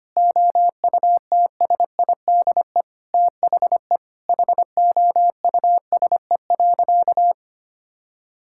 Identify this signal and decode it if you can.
Morse code, heard as a keyed tone, about 25 wpm, 700 Hz. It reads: OUTSIDE THE HOUSE.
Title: Sentences from Top 1000 Words - Review Part 6 - 25wpm